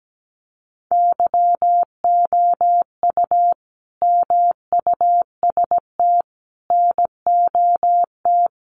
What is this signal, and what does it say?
Morse code, heard as a keyed tone, about 17 wpm, 700 Hz. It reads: YOU MUST NOT
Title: Sentences from Top 1300 Words 17wpm